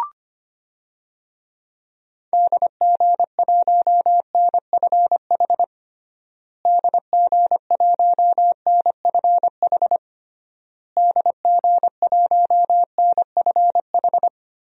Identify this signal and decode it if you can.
Morse code, heard as a keyed tone, about 25 wpm, 700 Hz. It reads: DG1NFH DG1NFH DG1NFH
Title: DXCC-100 Call Signs - 6 Characters - Repeated 3x - 25wpm